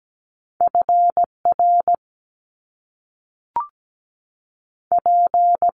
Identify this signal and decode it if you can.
Morse code, heard as a keyed tone, about 17 wpm, 700 Hz. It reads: F R P
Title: Top Two Letter Combinations in Words 17wpm